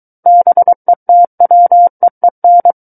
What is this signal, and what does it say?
Morse code, heard as a keyed tone, about 23 wpm, 700 Hz. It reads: BETWEEN
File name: Book One / 14-Chapter 14 - 23 WPM